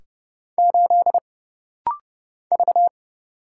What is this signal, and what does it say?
Morse code, heard as a keyed tone, about 30 wpm, 700 Hz. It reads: 8 V
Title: Single Letter-Number - Rapid-Fire 30wpm